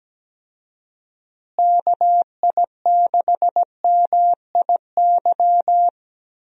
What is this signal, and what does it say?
Morse code, heard as a keyed tone, about 17 wpm, 700 Hz. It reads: KI6MIY
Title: USA Call Signs - 6 Characters - Repeated 2x - 17wpm